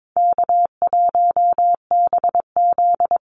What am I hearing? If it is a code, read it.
Morse code, heard as a keyed tone, about 22 wpm, 700 Hz. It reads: K1BZ
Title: Call Signs - 22wpm